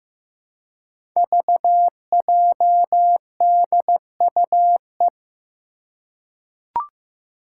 Morse code, heard as a keyed tone, about 15 wpm, 700 Hz. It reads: VJDUE
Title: Five Challenging Letter-Number Combinations 15wpm